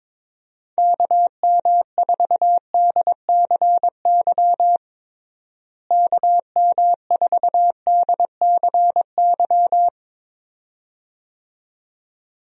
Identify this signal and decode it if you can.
Morse code, heard as a keyed tone, about 22 wpm, 700 Hz. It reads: KM4DCY KM4DCY
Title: USA Call Signs - 6 Characters - Repeated 2x - 22wpm